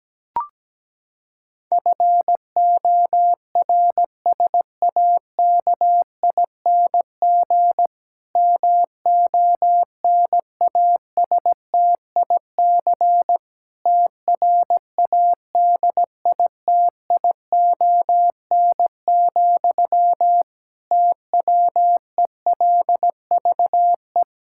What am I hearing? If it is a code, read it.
Morse code, heard as a keyed tone, about 17 wpm, 700 Hz. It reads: FORSAKING MONASTIC TRADITION, TWELVE
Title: Pangram Challenge v2 - 17wpm